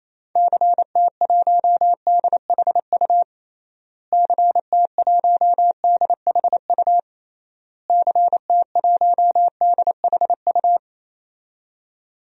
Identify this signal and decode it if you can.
Morse code, heard as a keyed tone, about 28 wpm, 700 Hz. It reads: CT1DHU CT1DHU CT1DHU
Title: DXCC-100 Call Signs - 6 Characters - Repeated 3x - 28wpm